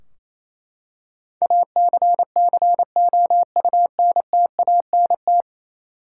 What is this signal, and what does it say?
Morse code, heard as a keyed tone, about 28 wpm, 700 Hz. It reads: ACCOUNTANT
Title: Practice QSO - Exchange 3 - More Info - 28wpm